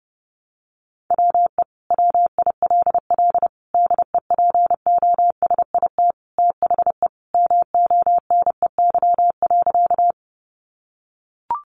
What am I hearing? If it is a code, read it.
Morse code, heard as a keyed tone, about 28 wpm, 700 Hz. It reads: WE WILL DEPOSIT THE MONEY.
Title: Sentences from Top 1300 Words 30wpm